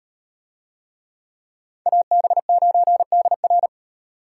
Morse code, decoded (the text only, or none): AB9DR